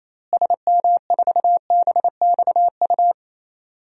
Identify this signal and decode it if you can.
Morse code, heard as a keyed tone, about 28 wpm, 700 Hz. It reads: SM4BXU
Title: DXCC-100 Call Signs - 6 Characters - Repeated 3x - 28wpm